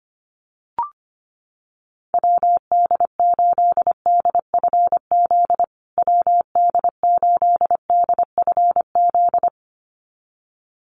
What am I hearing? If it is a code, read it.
Morse code, heard as a keyed tone, about 25 wpm, 700 Hz. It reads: WD8DFZ WD8DFZ
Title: USA Call Signs Repeated 2x 25wpm